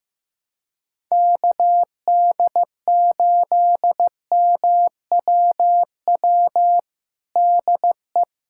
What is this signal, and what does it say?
Morse code, heard as a keyed tone, about 15 wpm, 700 Hz. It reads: KD8MWW DE
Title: USA Callsign DE Callsign - 6 Characters - 15wpm